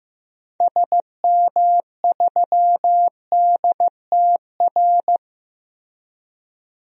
Morse code, heard as a keyed tone, about 15 wpm, 700 Hz. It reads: SM3DTR
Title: DXCC-100 Call Signs - 6 Characters - Repeated 3x - 15wpm